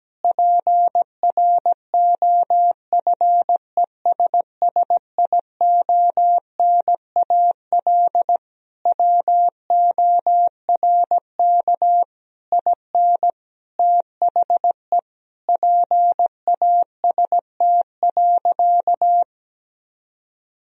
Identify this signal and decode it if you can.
Morse code, heard as a keyed tone, about 17 wpm, 700 Hz. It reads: PROFESSIONAL WORK IN THE PAST.
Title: Sentences from Top 1000 Words - Review Part 4 - 17wpm